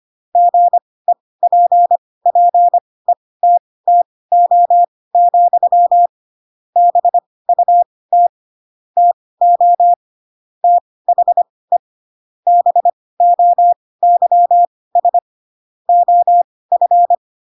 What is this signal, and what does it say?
Morse code, heard as a keyed tone, about 25 wpm, spaced as for 18 wpm, 700 Hz. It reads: GEPPETTO, BUT TO THE BOYS OF
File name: Pinocchio / Pinocchio ch2 12 25 ik1hge